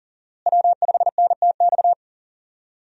Morse code, heard as a keyed tone, about 40 wpm, 700 Hz. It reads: W5NTX
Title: USA Call Signs - 5 Characters - Repeated 2x - 40wpm